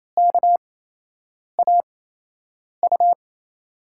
Morse code, heard as a keyed tone, about 28 wpm, spaced as for 8 wpm, 700 Hz. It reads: KAU
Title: USA Call Signs - 8x Character Spacing - 28wpm